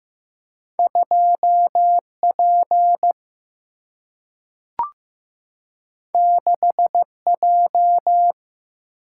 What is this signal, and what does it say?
Morse code, heard as a keyed tone, about 15 wpm, 700 Hz. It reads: 2P 6J
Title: Single-Letter Single-Number 15wpm